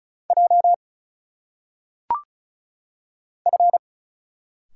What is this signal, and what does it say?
Morse code, heard as a keyed tone, about 34 wpm, 700 Hz. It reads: J F
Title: Single Letters - Phonetic Pronunciation 35wpm